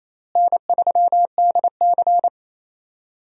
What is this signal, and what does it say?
Morse code, heard as a keyed tone, about 28 wpm, 700 Hz. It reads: N3DC
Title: USA Call Signs - 4 Characters - Repeated 2x - 28wpm